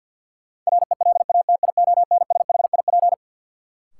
Morse code, heard as a keyed tone, about 52 wpm, 700 Hz. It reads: RELATIONSHIP
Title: Sentences from Top 1100 Words - New Word Drill Alt - 50wpm